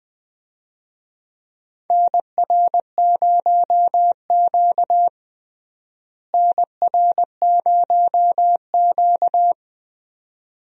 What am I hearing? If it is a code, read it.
Morse code, heard as a keyed tone, about 20 wpm, 700 Hz. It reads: NR0Q NR0Q
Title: USA Call Signs - 4 Characters - Repeated 2x - 20wpm